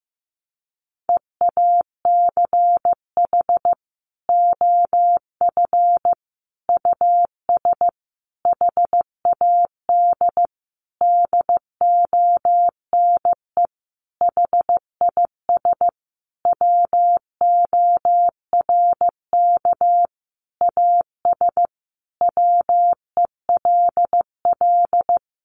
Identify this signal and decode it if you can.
Morse code, heard as a keyed tone, about 15 wpm, 700 Hz. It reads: EACH OF US HAD DONE HIS WORK AS WELL
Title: Sentences from Top 500 Words 15wpm